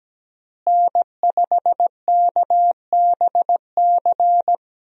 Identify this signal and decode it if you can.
Morse code, heard as a keyed tone, about 17 wpm, 700 Hz. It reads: N5KBC
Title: USA Call Signs - 5 Characters - Repeated 3x - 17wpm